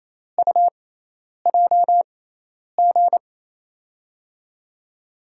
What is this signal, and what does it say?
Morse code, heard as a keyed tone, about 28 wpm, spaced as for 10 wpm, 700 Hz. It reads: UJG
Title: USA Call Signs - 6 Characters - 6x Character Spacing - 28wpm